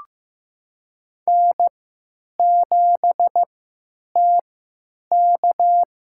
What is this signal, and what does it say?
Morse code, heard as a keyed tone, about 15 wpm, spaced as for 9 wpm, 700 Hz. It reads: N7TK
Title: USA Call Signs - 3x Character Spacing - 15wpm